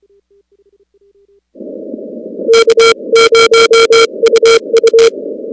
Morse code, heard as a keyed tone, about 25 wpm, 460 Hz. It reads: K0UU
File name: contests / 2019-cw-wpx-kt5j-k5pi- -2330 0-19